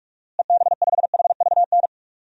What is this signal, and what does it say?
Morse code, heard as a keyed tone, about 43 wpm, 700 Hz. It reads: EB5HVN
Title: DXCC-100 Call Signs - 6 Characters - Repeated 3x - 45wpm